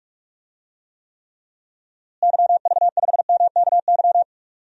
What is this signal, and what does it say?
Morse code, heard as a keyed tone, about 46 wpm, 700 Hz. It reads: YV5MKY
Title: DXCC-100 Call Signs - 6 Characters - Repeated 3x - 45wpm